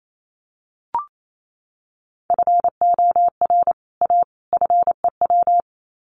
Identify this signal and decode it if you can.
Morse code, heard as a keyed tone, about 28 wpm, 700 Hz. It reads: FOR A FEW
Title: Sets of 3 Words 28wpm